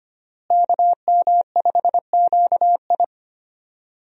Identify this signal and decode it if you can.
Morse code, heard as a keyed tone, about 25 wpm, 700 Hz. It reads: KM5QI
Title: USA Call Signs - 5 Characters - Repeated 3x - 25wpm